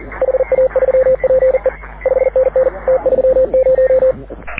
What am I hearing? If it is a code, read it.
Morse code, heard as a keyed tone, about 42 wpm, 520 Hz. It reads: HA3PE 5NN T30
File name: AUDIO2019 / HG6N CW SSB MIX